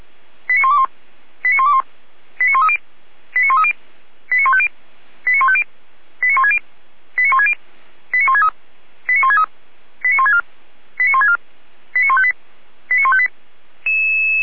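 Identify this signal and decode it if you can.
Five-tone selective call (ZVEI), heard as quick runs of five tones, tones of about 80 ms. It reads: 89121 89121 89130 89130 89150 89150 89160 89160 89163 89163 89164 89164 89158 89158
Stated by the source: ZVEI five-tone selective call 89140, received by SDR, Feuerwehr Orlamünde.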